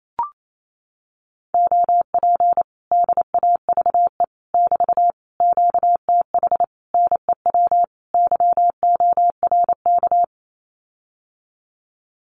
Morse code, heard as a keyed tone, about 28 wpm, 700 Hz. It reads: OP DAVE <BT> QTH NEW YORK
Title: Common QSO Phrases Encore 28wpm